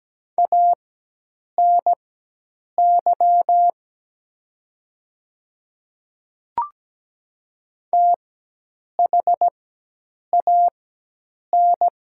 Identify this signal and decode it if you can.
Morse code, heard as a keyed tone, about 17 wpm, spaced as for 8 wpm, 700 Hz. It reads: ANY THAN
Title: 4x Character Spacing - Top 200 Words - 17wpm